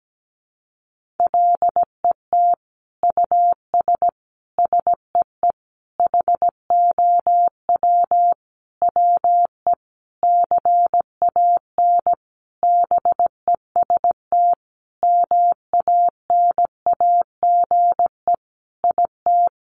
Morse code, heard as a keyed tone, about 17 wpm, 700 Hz. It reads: LET US SEE HOW WE CAN BEST MANAGE IT
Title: Sentences from Top 1400 Words 17wpm